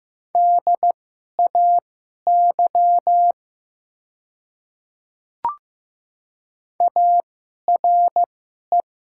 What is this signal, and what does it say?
Morse code, heard as a keyed tone, about 15 wpm, spaced as for 11 wpm, 700 Hz. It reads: DAY ARE
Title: 2x Character Spacing - Top 300 Words - 15wpm